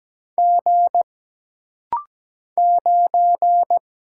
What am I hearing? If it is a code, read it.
Morse code, heard as a keyed tone, about 17 wpm, 700 Hz. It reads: G 9
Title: Single Letter-Number Alt - Rapid-Fire 17wpm